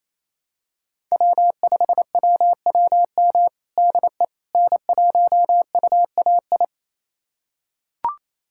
Morse code, heard as a keyed tone, about 28 wpm, 690 Hz. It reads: W5WWM DE N1UAI
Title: USA Callsign DE Callsign - 5 Characters - 28wpm